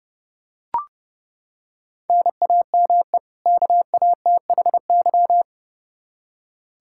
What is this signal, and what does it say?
Morse code, heard as a keyed tone, about 32 wpm, 700 Hz. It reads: NAME KATHY
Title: Practice QSO - Exchange 1 - Beginning - 30wpm